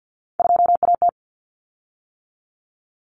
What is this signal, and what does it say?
Morse code, heard as a keyed tone, about 50 wpm, 700 Hz. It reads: 2AT
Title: Common Cut Numbers - 3-Digit 50wpm